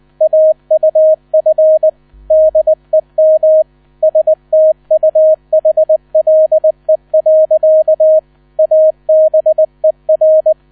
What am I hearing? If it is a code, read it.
Morse code, heard as a keyed tone, about 19 wpm, 630 Hz. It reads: AUF DEM STUHLE. ABER
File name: 000 Vorwort 20wpm